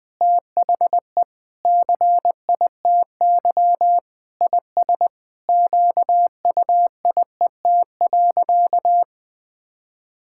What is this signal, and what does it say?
Morse code, heard as a keyed tone, about 20 wpm, 700 Hz. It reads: THE CITY IS QUIET.
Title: Sentences with 7-Word Phrases - Spoken Beginning - 20wpm